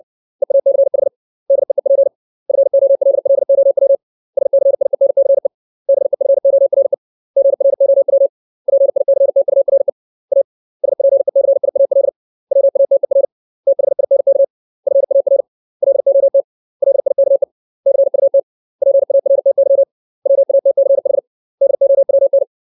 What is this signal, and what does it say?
Morse code, heard as a keyed tone, about 60 wpm, 550 Hz. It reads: EACH DEEP FOLLOW SCIENCE BEFORE GROW PICTURE A SPECIAL WATER THINK FAR LOT LIKE CUT PARTY WATCH DOWN
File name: 632 words / 632 Words 60WPM